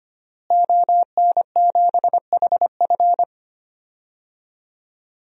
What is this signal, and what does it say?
Morse code, heard as a keyed tone, about 25 wpm, 700 Hz. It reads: ON7HF
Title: DXCC-100 Call Signs - 5 Characters - Repeated 3x - 25wpm